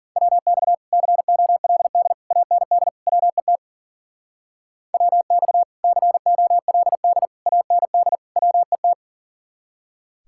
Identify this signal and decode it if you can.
Morse code, heard as a keyed tone, about 48 wpm, 700 Hz. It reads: WX COLD AND WET WX COLD AND WET
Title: Speed-Racing Common QSO Phrases - 40wpm